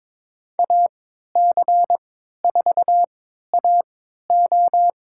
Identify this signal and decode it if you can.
Morse code, heard as a keyed tone, about 22 wpm, spaced as for 13 wpm, 700 Hz. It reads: AC4AO
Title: USA Call Signs - 5 Characters - 3x Character Spacing - 22wpm